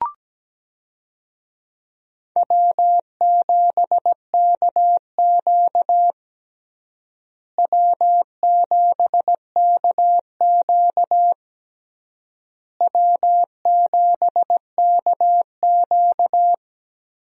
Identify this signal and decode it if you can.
Morse code, heard as a keyed tone, about 17 wpm, 700 Hz. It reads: W7KQ W7KQ W7KQ
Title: USA Call Signs - 4 Characters - Repeated 3x - 17wpm